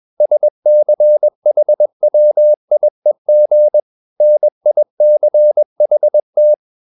Morse code, heard as a keyed tone, about 21 wpm, 600 Hz. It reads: SCHWIEG NICHT